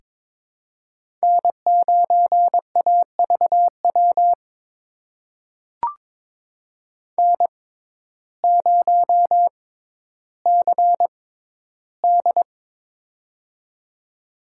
Morse code, heard as a keyed tone, about 22 wpm, spaced as for 12 wpm, 700 Hz. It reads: N9AVW N0CD
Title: USA Call Signs - 6x Character Spacing - 22wpm